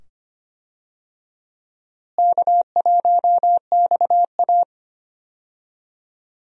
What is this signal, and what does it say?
Morse code, heard as a keyed tone, about 25 wpm, 700 Hz. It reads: K1XA
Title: USA Call Signs - 4 Characters - Repeated 2x - 25wpm